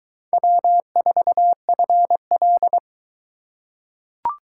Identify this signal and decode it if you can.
Morse code, heard as a keyed tone, about 22 wpm, 700 Hz. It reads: W4FL